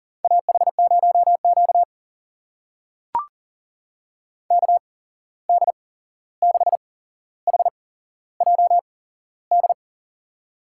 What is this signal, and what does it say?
Morse code, heard as a keyed tone, about 40 wpm, spaced as for 15 wpm, 700 Hz. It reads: AH0Q KD6HJD
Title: USA Call Signs - 8x Character Spacing - 40wpm